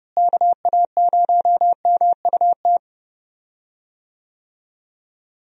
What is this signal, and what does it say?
Morse code, heard as a keyed tone, about 31 wpm, 700 Hz. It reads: KA0MUT
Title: USA Call Signs - 6 Characters - Repeated 3x - 30wpm